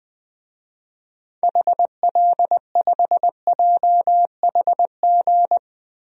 Morse code, heard as a keyed tone, about 20 wpm, 700 Hz. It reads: HL5JHG
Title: DXCC-100 Call Signs - 6 Characters - 20wpm